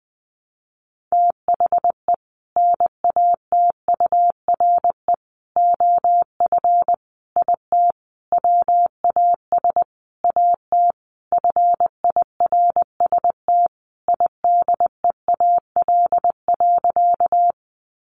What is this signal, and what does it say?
Morse code, heard as a keyed tone, about 20 wpm, 700 Hz. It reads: THE NATURE OF IT WAS AT FIRST IDEAL.
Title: Sentences from Top 1400 Words 20wpm